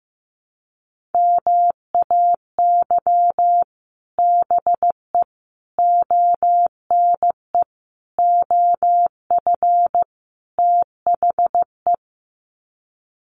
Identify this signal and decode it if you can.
Morse code, heard as a keyed tone, about 15 wpm, 700 Hz. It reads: MAY BE ONE OF THE
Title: Sets of 5 Words Encore2 15wpm